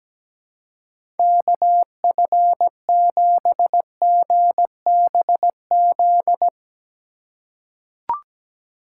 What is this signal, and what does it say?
Morse code, heard as a keyed tone, about 17 wpm, 700 Hz. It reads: KF7GBZ
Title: USA Call Signs - 6 Characters - 2x Character Spacing - 17wpm